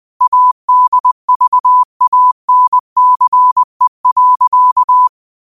Morse code, heard as a keyed tone, about 20 wpm, 1000 Hz. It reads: ADVANCE.
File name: qotdfiles / 20wpm